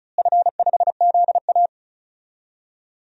Morse code, heard as a keyed tone, about 34 wpm, 700 Hz. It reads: F5ZA
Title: DXCC-100 Call Signs - 4 Characters - Repeated 3x - 35wpm